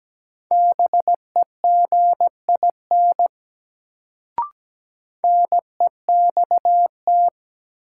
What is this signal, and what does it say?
Morse code, heard as a keyed tone, about 17 wpm, 700 Hz. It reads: BEGIN NEXT
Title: Sentences from Top 300 Words - New Word Drill - 17wpm